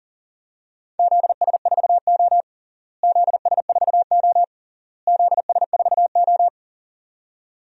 Morse code, heard as a keyed tone, about 40 wpm, 700 Hz. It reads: ZS4O ZS4O ZS4O
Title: DXCC-100 Call Signs - 4 Characters - Repeated 3x - 40wpm